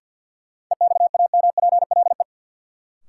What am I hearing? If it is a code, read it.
Morse code, heard as a keyed tone, about 47 wpm, 700 Hz. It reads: EXAMPLE
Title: Sets of 2 Words - New Word Drill - 50wpm